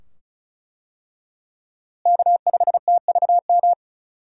Morse code, heard as a keyed tone, about 34 wpm, 700 Hz. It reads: K5TVM
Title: USA Call Signs - 5 Characters - Repeated 2x - 35wpm